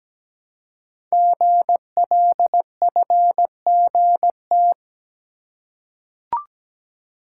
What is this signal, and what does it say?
Morse code, heard as a keyed tone, about 17 wpm, 700 Hz. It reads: GLFGT
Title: Code Groups - 5 Letters - 17wpm